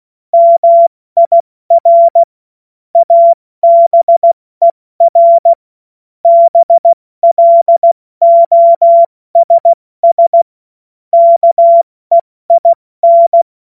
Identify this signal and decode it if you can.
Morse code, heard as a keyed tone, about 16 wpm, 680 Hz. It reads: MIR ABER BLOSS KEIN
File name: Morsekurs / Witze / Witz 07. September 2021, 70 F80